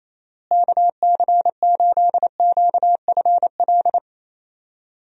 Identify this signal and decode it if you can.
Morse code, heard as a keyed tone, about 28 wpm, 700 Hz. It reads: KC8QFL
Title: USA Call Signs - 6 Characters - Repeated 3x - 28wpm